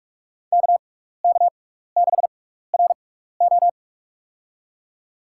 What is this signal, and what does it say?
Morse code, heard as a keyed tone, about 42 wpm, spaced as for 15 wpm, 700 Hz. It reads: KK6RO
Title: USA Call Signs - 6x Character Spacing - 45wpm